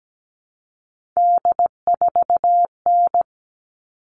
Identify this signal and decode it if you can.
Morse code, heard as a keyed tone, about 17 wpm, 700 Hz. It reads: D4N
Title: All Cut Numbers - 3-Digit 17wpm